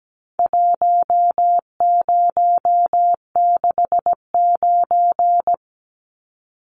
Morse code, heard as a keyed tone, about 17 wpm, 700 Hz. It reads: 1069
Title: Numbers - Contest Style - 4-Digit 17wpm